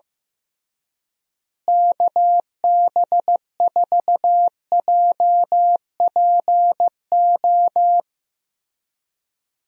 Morse code, heard as a keyed tone, about 16 wpm, 700 Hz. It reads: KB4JPO
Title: USA Call Signs - 6 Characters - Repeated 3x - 15wpm